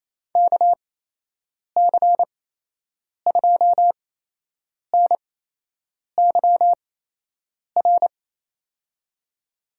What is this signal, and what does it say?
Morse code, heard as a keyed tone, about 28 wpm, spaced as for 8 wpm, 700 Hz. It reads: KC2NYR
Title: USA Call Signs - 6 Characters - 8x Character Spacing - 28wpm